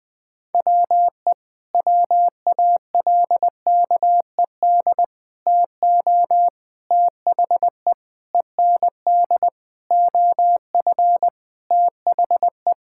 Morse code, heard as a keyed tone, about 20 wpm, 700 Hz. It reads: WE WALKED TO THE END OF THE
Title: Sentences with 5-Word Phrases - 20wpm